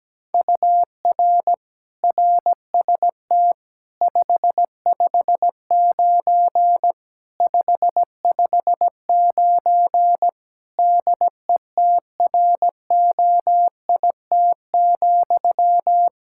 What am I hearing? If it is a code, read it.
Morse code, heard as a keyed tone, about 17 wpm, 700 Hz. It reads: UR RST 559 559 DETROIT,
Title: Practice QSO - Exchange 1 - Beginning - 17wpm